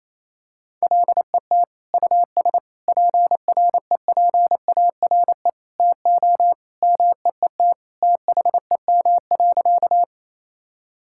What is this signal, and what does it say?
Morse code, heard as a keyed tone, about 28 wpm, 700 Hz. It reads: LET US PREPARE TO MEET THEM.
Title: Sentences from Top 1000 Words - Review Part 6 - 28wpm